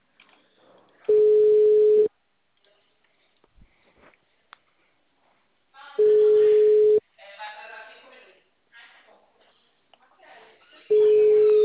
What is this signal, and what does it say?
Ringback tone (Europe, 425 Hz), with speech.